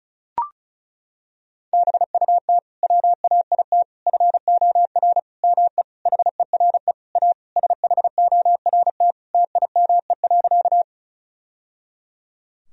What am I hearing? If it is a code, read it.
Morse code, heard as a keyed tone, about 36 wpm, 700 Hz. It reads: BUT WAIT FOR ME HERE A SHORT TIME.
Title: Sentences from Top 1000 Words - Review Part 2 - 35wpm